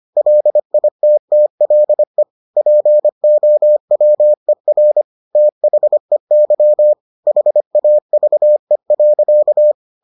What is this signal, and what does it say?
Morse code, heard as a keyed tone, about 25 wpm, 600 Hz. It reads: LITTLE POWER THEY HAVE.